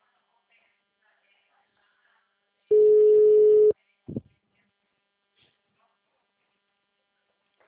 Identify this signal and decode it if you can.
Ringback tone (Europe, 425 Hz), with speech.